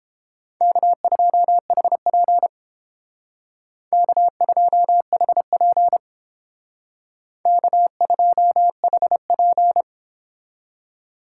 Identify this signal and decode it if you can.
Morse code, heard as a keyed tone, about 30 wpm, 700 Hz. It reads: K2HP K2HP K2HP